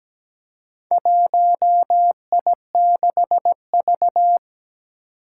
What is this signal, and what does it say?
Morse code, heard as a keyed tone, about 17 wpm, 700 Hz. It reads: 1I6V
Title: Four Challenging Letter-Number Combinations 17wpm